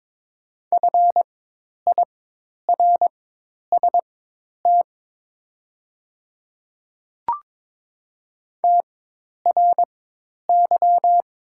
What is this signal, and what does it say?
Morse code, heard as a keyed tone, about 22 wpm, spaced as for 10 wpm, 690 Hz. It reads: FIRST TRY
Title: 4x Character Spacing - Top 200 Words - 22wpm